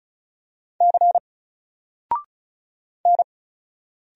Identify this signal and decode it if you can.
Morse code, heard as a keyed tone, about 35 wpm, 700 Hz. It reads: C N
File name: Single-Letter-Number-35wpm